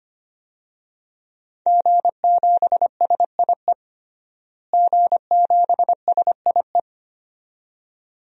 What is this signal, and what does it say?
Morse code, heard as a keyed tone, about 25 wpm, 700 Hz. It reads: G7SIE G7SIE
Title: DXCC-100 Call Signs - 5 Characters - Repeated 2x - 25wpm